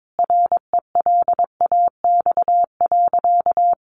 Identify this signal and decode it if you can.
Morse code, heard as a keyed tone, about 22 wpm, 700 Hz. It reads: RELAX.
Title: A Sound Of Thunder 22wpm